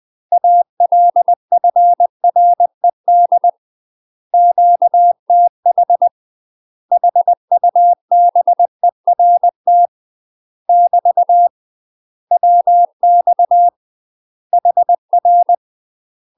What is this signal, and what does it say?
Morse code, heard as a keyed tone, about 20 wpm, 700 Hz. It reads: ALFRED QTH HUBERT <BT> WX HR
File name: UK MASTER QSOs TRAINING / 20WPM / UK MASTER QSOs 0016